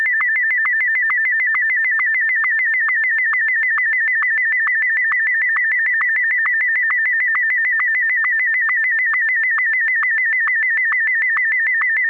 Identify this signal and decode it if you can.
SSTV (slow-scan television), Martin 1.